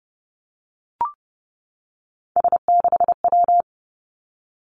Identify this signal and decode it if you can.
Morse code, heard as a keyed tone, about 30 wpm, 700 Hz. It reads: S6W
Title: Three Challenging Letter-Number Combinations 30wpm